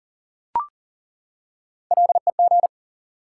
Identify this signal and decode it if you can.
Morse code, heard as a keyed tone, about 40 wpm, 700 Hz. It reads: LEG
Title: Three Letter Words - 40wpm